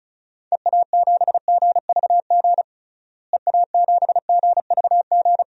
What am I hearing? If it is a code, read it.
Morse code, heard as a keyed tone, about 36 wpm, 700 Hz. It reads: EA7GVG EA7GVG
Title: DXCC-100 Call Signs - 6 Characters - Repeated 3x - 35wpm